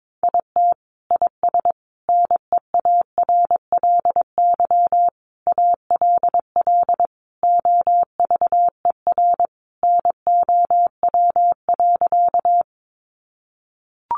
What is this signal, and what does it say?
Morse code, heard as a keyed tone, about 22 wpm, 700 Hz. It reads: IT IS NEARLY ALL OVER NOW.
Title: Sentences from Top 1300 Words 22wpm